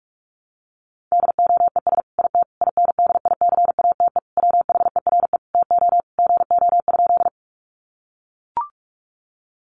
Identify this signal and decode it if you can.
Morse code, heard as a keyed tone, about 44 wpm, 700 Hz. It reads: DOES IT INDICATE WHERE TO GO?
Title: Sentences from Top 1300 Words 45wpm